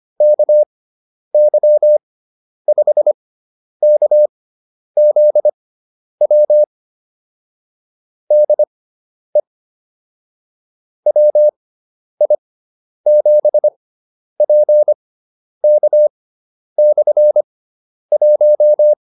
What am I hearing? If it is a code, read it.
Morse code, heard as a keyed tone, about 25 wpm, spaced as for 10 wpm, 600 Hz. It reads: KY5KZW DE WI7PK/1
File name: QSO304 10